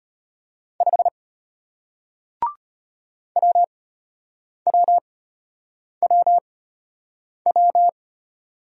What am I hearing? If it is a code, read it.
Morse code, heard as a keyed tone, about 34 wpm, spaced as for 8 wpm, 700 Hz. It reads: 5 WWWW